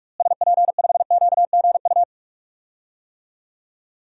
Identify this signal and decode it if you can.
Morse code, heard as a keyed tone, about 42 wpm, 700 Hz. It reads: SP5QGU
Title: DXCC-100 Call Signs - 6 Characters - Repeated 3x - 45wpm